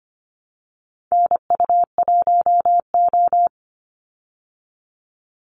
Morse code, heard as a keyed tone, about 25 wpm, 700 Hz. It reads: NU1O
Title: Call Signs Repeated 2x - 25wpm